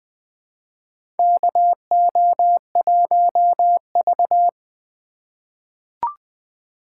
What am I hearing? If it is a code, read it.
Morse code, heard as a keyed tone, about 20 wpm, 700 Hz. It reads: KO1V
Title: USA Call Signs - 3x Character Spacing - 20wpm